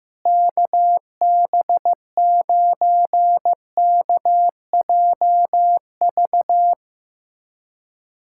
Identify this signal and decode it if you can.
Morse code, heard as a keyed tone, about 15 wpm, 700 Hz. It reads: KB9KJV
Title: USA Callsign DE Callsign - 6 Characters - 15wpm